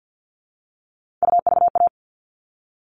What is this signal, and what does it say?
Morse code, heard as a keyed tone, about 48 wpm, 700 Hz. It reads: UVA
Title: All Cut Numbers - 3-Digit 50wpm